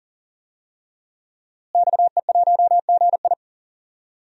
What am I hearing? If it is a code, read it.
Morse code, heard as a keyed tone, about 40 wpm, 700 Hz. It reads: XE1GI
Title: DXCC-100 Call Signs - 5 Characters - Repeated 2x - 40wpm